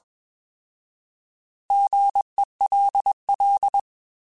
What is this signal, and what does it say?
Morse code, heard as a keyed tone, about 21 wpm, 800 Hz. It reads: GELL
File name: Alphabetical / Alphabetical mp3s / Gs @20